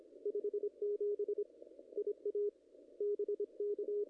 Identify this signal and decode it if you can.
Morse code, heard as a keyed tone, about 26 wpm, 410 Hz. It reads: 57 IA BK
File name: contests / ss-cw-k5tr-op-n5rz -0000